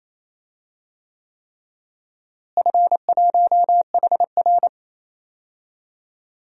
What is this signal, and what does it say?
Morse code, heard as a keyed tone, about 28 wpm, 700 Hz. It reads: F1HR
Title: DXCC-100 Call Signs - 4 Characters - Repeated 3x - 28wpm